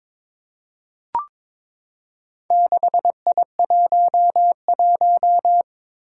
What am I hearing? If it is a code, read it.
Morse code, heard as a keyed tone, about 22 wpm, 700 Hz. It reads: 6I11
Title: Four Challenging Letter-Number Combinations 22wpm